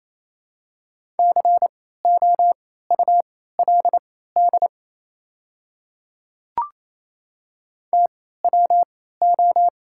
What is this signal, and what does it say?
Morse code, heard as a keyed tone, about 27 wpm, spaced as for 16 wpm, 700 Hz. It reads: COULD TWO
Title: 3x Character Spacing - Top 200 Words - 28wpm